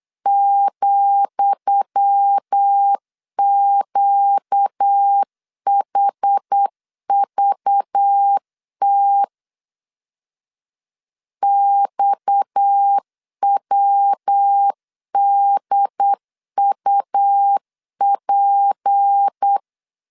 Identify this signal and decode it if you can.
Morse code, heard as a keyed tone, about 8 wpm, 800 Hz. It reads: ,QHVT XWDUP
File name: LEZIONI / 10 SIGNS / 10D COMA 8wpm 3749